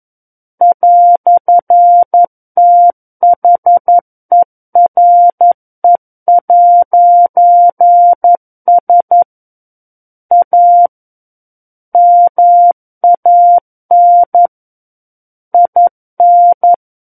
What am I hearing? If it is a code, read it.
Morse code, heard as a keyed tone, about 11 wpm, 700 Hz. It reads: "THERE'S A MAN IN
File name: 11 WPM